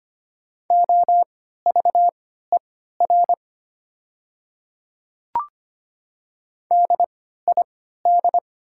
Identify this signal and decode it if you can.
Morse code, heard as a keyed tone, about 25 wpm, spaced as for 14 wpm, 690 Hz. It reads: OVER DID
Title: 3x Character Spacing - Top 200 Words - 25wpm